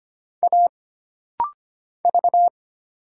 Morse code, heard as a keyed tone, about 25 wpm, 700 Hz. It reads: A V
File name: Single Character (Letter:Number:Punctuation:Prosign) - Rapid-Fire - 25wpm